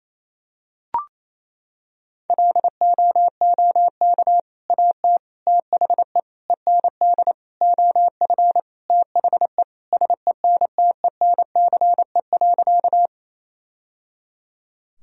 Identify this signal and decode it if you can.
Morse code, heard as a keyed tone, about 27 wpm, 700 Hz. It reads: LOOK AT THE END OF THE SENTENCE.
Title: Sentences from Top 1000 Words - Review Part 1 - 28wpm